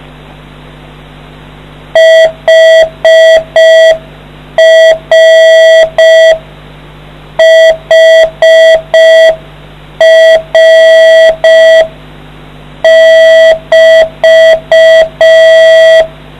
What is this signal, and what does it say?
Morse code, heard as a keyed tone, about 5 wpm, 650 Hz. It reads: HR HR <BT>